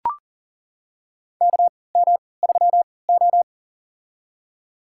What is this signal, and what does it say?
Morse code, heard as a keyed tone, about 40 wpm, spaced as for 23 wpm, 700 Hz. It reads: KM3O
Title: USA Call Signs - 4 Characters - 3x Character Spacing - 40wpm